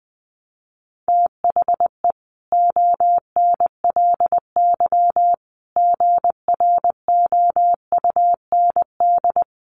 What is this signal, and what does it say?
Morse code, heard as a keyed tone, about 20 wpm, 700 Hz. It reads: THE ONLY GROUND
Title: Sentences from Top 1400 Words 20wpm